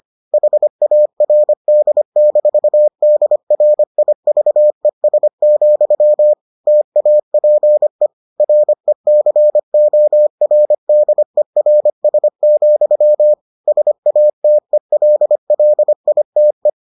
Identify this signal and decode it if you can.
Morse code, heard as a keyed tone, about 25 wpm, 600 Hz. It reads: HARD-DRIVES, TAPE RECORDERS, SATELLITE